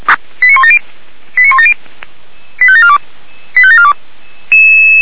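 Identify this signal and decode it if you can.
Five-tone selective call (ZVEI), heard as quick runs of five tones, tones of about 70 ms. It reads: 89170 89170 85742 85742
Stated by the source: ZVEI five-tone selective call received on an SDR.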